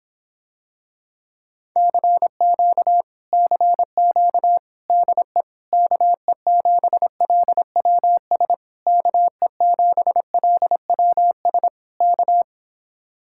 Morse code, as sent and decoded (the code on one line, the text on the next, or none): -.-. --.- / -.-. --.- / -.. / -.- . --... .-.. .-- ... / -.- . --... .-.. .-- ... / -.-
CQ CQ DE KE7LWS KE7LWS K